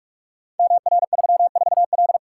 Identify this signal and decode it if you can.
Morse code, heard as a keyed tone, about 46 wpm, 700 Hz. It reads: MR34L
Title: Code Groups - 5 Letter:Numbers - 45wpm